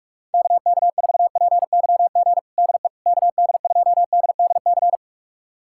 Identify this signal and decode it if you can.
Morse code, heard as a keyed tone, about 46 wpm, 700 Hz. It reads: KK4PYG DE KD2DDC
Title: USA Callsign DE Callsign 45wpm